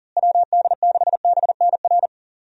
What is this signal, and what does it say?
Morse code, heard as a keyed tone, about 37 wpm, 700 Hz. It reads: WD6BNR
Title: USA Call Signs - 6 Characters - Repeated 3x - 40wpm